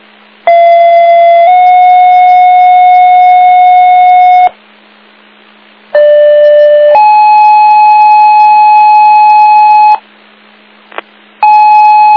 Two-tone page, heard as a one-second tone, then a three-second tone, about 680 then 730 Hz, about 590 then 850 Hz. It is heard from 0.5 to 9.9 s.